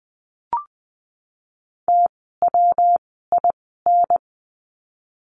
Morse code, heard as a keyed tone, about 20 wpm, spaced as for 15 wpm, 700 Hz. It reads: TWIN
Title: 115 Lesson 19f - Letter w - Words 20wpm